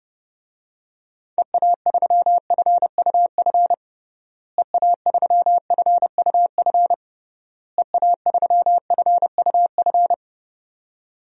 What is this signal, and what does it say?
Morse code, heard as a keyed tone, about 30 wpm, 700 Hz. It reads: EA3FUF EA3FUF EA3FUF
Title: DXCC-100 Call Signs - 6 Characters - Repeated 3x - 30wpm